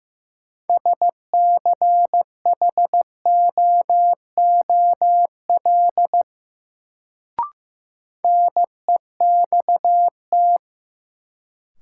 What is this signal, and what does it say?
Morse code, heard as a keyed tone, about 16 wpm, 700 Hz. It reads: SCHOOL NEXT
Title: Sentences from Top 300 Words - New Word Drill - 15wpm